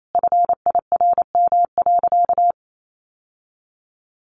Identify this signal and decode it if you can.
Morse code, heard as a keyed tone, about 28 wpm, 700 Hz. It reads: FIRM.
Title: Sentences from Top 1400 Words 28wpm